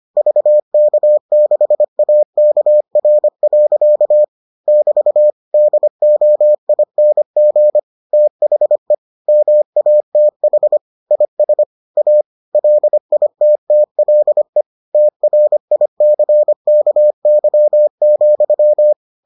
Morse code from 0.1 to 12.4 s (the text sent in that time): VK6AKR. <BT> DOING THE MATH IS A